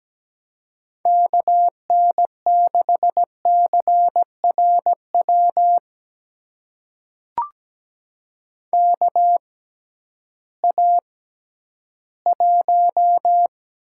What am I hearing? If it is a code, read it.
Morse code, heard as a keyed tone, about 17 wpm, spaced as for 11 wpm, 700 Hz. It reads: KN6CRW KA1
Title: USA Call Signs - 6 Characters - 6x Character Spacing - 17wpm